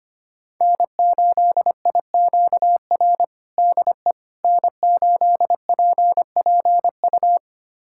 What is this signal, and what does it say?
Morse code, heard as a keyed tone, about 25 wpm, 700 Hz. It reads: N8IQR DE N8PPU
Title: USA Callsign DE Callsign - 5 Characters - 25wpm